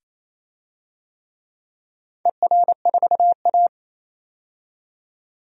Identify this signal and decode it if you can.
Morse code, heard as a keyed tone, about 28 wpm, 700 Hz. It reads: ER4A
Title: DXCC-100 Call Signs - 4 Characters - 28wpm